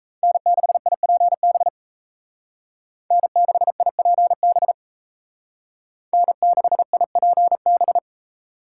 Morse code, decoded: N6IPB N6IPB N6IPB

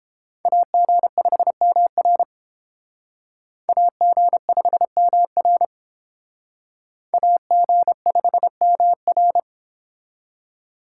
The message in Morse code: AG5MR AG5MR AG5MR

